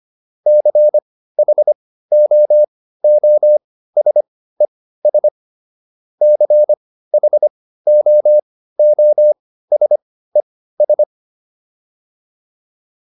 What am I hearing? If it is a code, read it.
Morse code, heard as a keyed tone, about 25 wpm, spaced as for 15 wpm, 600 Hz. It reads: CHOOSES CHOOSES